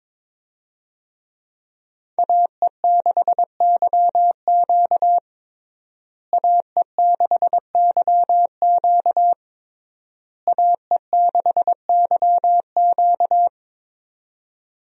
Morse code, heard as a keyed tone, about 22 wpm, 700 Hz. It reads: AE6YQ AE6YQ AE6YQ
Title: USA Call Signs - 5 Characters - Repeated 3x - 22wpm